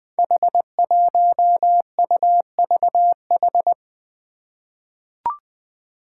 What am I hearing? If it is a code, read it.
Morse code, heard as a keyed tone, about 21 wpm, 700 Hz. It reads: H1UVH
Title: Five Challenging Letter-Number Combinations 20wpm